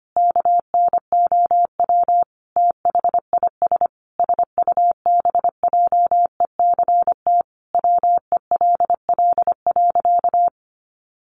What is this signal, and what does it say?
Morse code, heard as a keyed tone, about 25 wpm, 700 Hz. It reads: KNOW THIS SUBJECT WELL.
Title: Sentences from Top 800 Words 25wpm